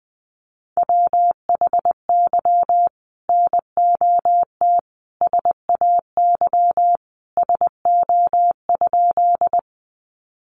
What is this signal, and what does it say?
Morse code, heard as a keyed tone, about 20 wpm, 700 Hz. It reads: WHY NOT SAY SO?
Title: Sentences from Top 600 Words 20wpm